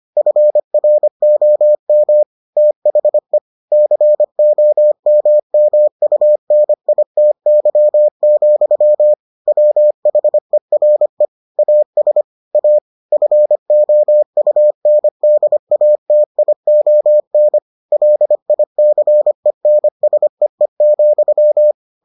Morse code, heard as a keyed tone, about 25 wpm, 600 Hz. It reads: FROM THE COMMUNITY, WHERE AS A FOUNDATION LICENSEE,